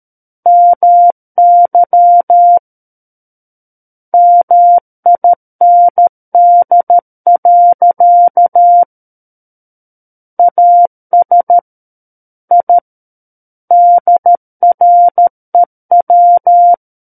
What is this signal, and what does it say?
Morse code, heard as a keyed tone, about 13 wpm, 700 Hz. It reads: MY MIND. AS I DREW
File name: Book One / 04-Chapter 4 - 13 WPM